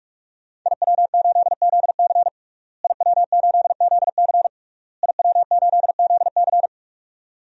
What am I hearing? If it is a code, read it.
Morse code, heard as a keyed tone, about 46 wpm, 700 Hz. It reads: IW8ZC IW8ZC IW8ZC
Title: DXCC-100 Call Signs - 5 Characters - Repeated 3x - 45wpm